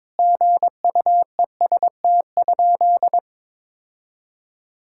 Morse code, heard as a keyed tone, about 22 wpm, 700 Hz. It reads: GUEST?
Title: Sentences from Top 1600 Words 22wpm